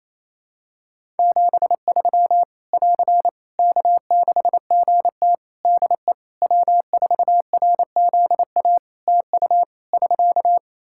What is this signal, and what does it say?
Morse code, heard as a keyed tone, about 28 wpm, 700 Hz. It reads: 73 <AR> K6GT DE W4RZA TU <SK>
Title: Practice QSO - Exchange 4 - Ending - 28wpm